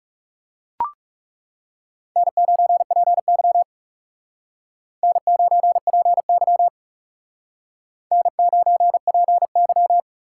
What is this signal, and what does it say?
Morse code, heard as a keyed tone, about 39 wpm, 700 Hz. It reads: N9PY N9PY N9PY